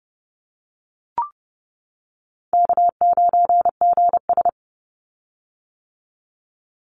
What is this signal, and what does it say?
Morse code, heard as a keyed tone, about 30 wpm, 700 Hz. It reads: K9GS
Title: Call Signs - 30wpm